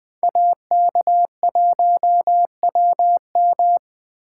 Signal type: Morse code

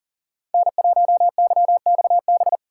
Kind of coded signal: Morse code

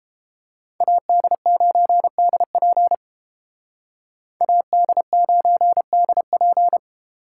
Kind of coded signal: Morse code

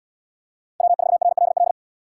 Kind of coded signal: Morse code

